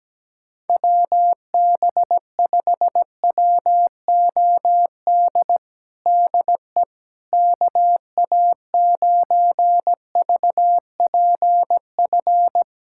Morse code, heard as a keyed tone, about 17 wpm, 700 Hz. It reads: WB5WOD DE KA9VPF
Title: USA Callsign DE Callsign - 6 Characters - 17wpm